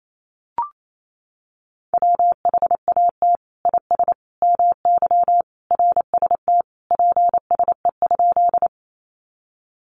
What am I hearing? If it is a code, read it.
Morse code, heard as a keyed tone, about 28 wpm, 700 Hz. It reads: WHAT IS MY RST PSE?
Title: Common QSO Phrases 28wpm